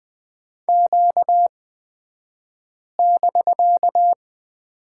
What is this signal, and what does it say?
Morse code, heard as a keyed tone, about 20 wpm, spaced as for 5 wpm, 700 Hz. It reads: Q<BK>
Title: 249 Lesson 40d - BK - Two Characters 20wpm